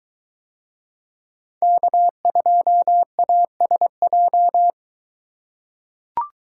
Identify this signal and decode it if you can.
Morse code, heard as a keyed tone, about 22 wpm, 700 Hz. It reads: K2ASJ